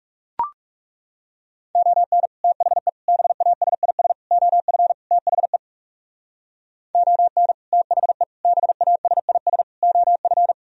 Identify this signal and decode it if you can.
Morse code, heard as a keyed tone, about 36 wpm, 700 Hz. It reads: ON THE BASIS OF THE ON THE BASIS OF
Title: Speed-Racing - Sets of 5 Words 30wpm